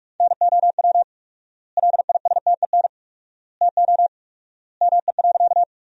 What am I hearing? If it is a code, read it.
Morse code, heard as a keyed tone, about 46 wpm, 700 Hz. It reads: NOW LISTEN TO ME.
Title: 4x Word-Spacing - Sentences from Top 400 Words - 45wpm